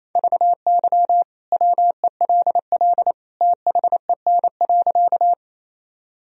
Morse code, heard as a keyed tone, about 28 wpm, 700 Hz. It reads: VY WELL THEN.
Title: Sentences from Top 200 Words and Abbreviations 28wpm